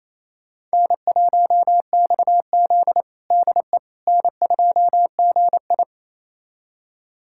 Morse code, heard as a keyed tone, about 28 wpm, 700 Hz. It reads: N1XZ DE N2GI
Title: USA Callsign DE Callsign - 4 Characters - 28wpm